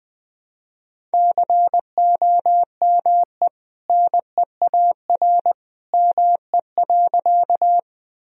Morse code, heard as a keyed tone, about 20 wpm, 700 Hz. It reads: COME NEAR ME.
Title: Sentences from Top 1000 Words - Review Part 1 - 20wpm